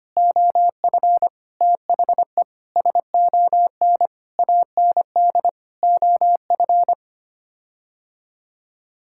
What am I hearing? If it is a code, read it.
Morse code, heard as a keyed tone, about 25 wpm, 700 Hz. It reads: OF THE SON AND OF
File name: Sets of 9 Words - 25wpm